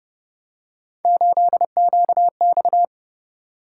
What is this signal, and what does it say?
Morse code, heard as a keyed tone, about 31 wpm, 700 Hz. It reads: 8QX